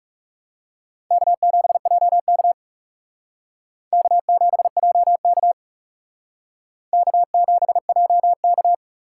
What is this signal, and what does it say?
Morse code, heard as a keyed tone, about 36 wpm, 700 Hz. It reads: K7JK K7JK K7JK
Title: Speed Racing - USA 1x2 Call Signs 30wpm